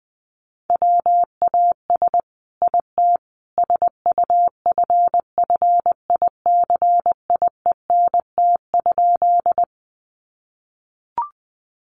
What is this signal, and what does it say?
Morse code, heard as a keyed tone, about 20 wpm, 700 Hz. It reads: WAS IT SUFFICIENT?
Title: Sentences from Top 1400 Words 20wpm